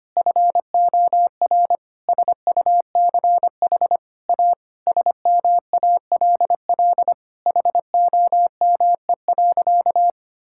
Morse code, heard as a keyed tone, about 25 wpm, 700 Hz. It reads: FOR SUCH A SMALL HOME.
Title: Sentences from Top 1000 Words - Review Part 6 - 25wpm